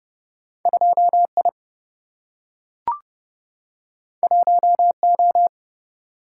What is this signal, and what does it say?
Morse code, heard as a keyed tone, about 30 wpm, 700 Hz. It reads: 2 I 1 O
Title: Single-Letter Single-Number 30wpm